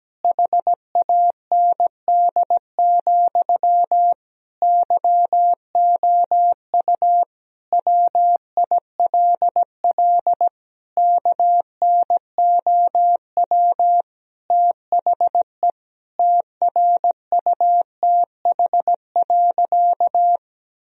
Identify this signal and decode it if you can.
Morse code, heard as a keyed tone, about 17 wpm, 700 Hz. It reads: HAND, YOU WILL KNOW THE TRUTH.
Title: Sentences from Top 1000 Words - Review Part 3 - 17wpm